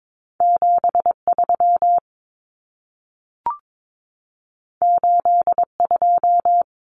Morse code, heard as a keyed tone, about 22 wpm, 700 Hz. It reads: 73 82
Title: Common Cut Numbers - 2-Digit 22wpm